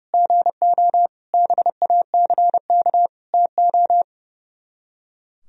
Morse code, heard as a keyed tone, about 37 wpm, 700 Hz. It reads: GO BACK TO
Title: Speed-Racing - Sets of 5 Words 30wpm